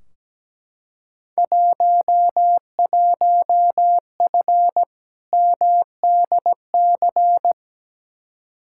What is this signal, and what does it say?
Morse code, heard as a keyed tone, about 17 wpm, 700 Hz. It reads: 11F MDC
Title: Field Day Exchanges Encore - 17wpm